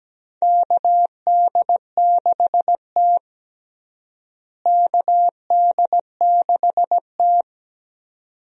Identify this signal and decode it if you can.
Morse code, heard as a keyed tone, about 17 wpm, 700 Hz. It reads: KD6T KD6T
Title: USA Call Signs - 4 Characters - Repeated 2x - 17wpm